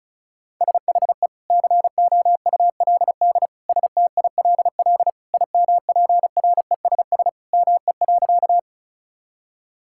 Morse code, heard as a keyed tone, about 36 wpm, 700 Hz. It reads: SHE COULD STILL IMPRESS ME.
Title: Sentences from Top 1600 Words 35wpm